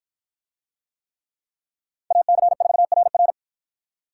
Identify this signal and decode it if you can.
Morse code, heard as a keyed tone, about 53 wpm, 700 Hz. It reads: AC4RR